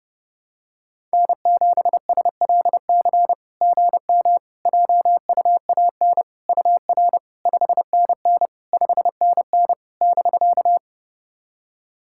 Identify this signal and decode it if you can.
Morse code, heard as a keyed tone, about 32 wpm, 700 Hz. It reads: N7SLC GM JUAN UR 5NN 5NN <BK>
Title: SOTA QSO Practice 30wpm